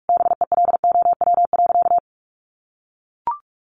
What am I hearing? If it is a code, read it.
Morse code, heard as a keyed tone, about 44 wpm, 700 Hz. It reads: BELOW.
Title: Sentences from Top 1500 Words 45wpm